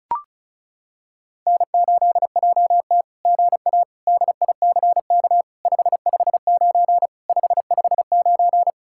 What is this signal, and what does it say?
Morse code, heard as a keyed tone, about 36 wpm, 700 Hz. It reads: N8JT GA DICK 559 559
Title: SOTA QSO Practice 35wpm